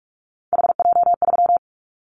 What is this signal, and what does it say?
Morse code, heard as a keyed tone, about 46 wpm, 700 Hz. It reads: HJ3
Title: Callsign Prefixes for Top 100 DXCC - 45wpm